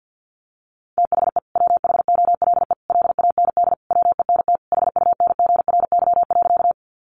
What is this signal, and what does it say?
Morse code, heard as a keyed tone, about 50 wpm, 700 Hz. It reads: THE WHOLE LAND WENT HUNGRY.
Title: Sentences from Top 1400 Words 50wpm